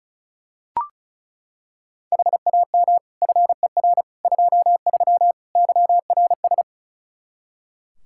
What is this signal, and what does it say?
Morse code, heard as a keyed tone, about 34 wpm, 700 Hz. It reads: HAM FER 23 YRS
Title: Practice QSO - Exchange 3 - More Info - 35wpm